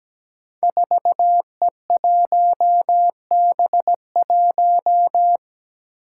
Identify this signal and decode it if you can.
Morse code, heard as a keyed tone, about 17 wpm, 700 Hz. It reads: 4E1B1
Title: Five Challenging Letter-Number Combinations 17wpm